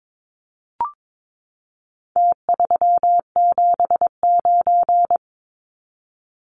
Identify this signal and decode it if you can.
Morse code, heard as a keyed tone, about 22 wpm, 700 Hz. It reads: T379
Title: Numbers - Contest Style - 4-Digit 22wpm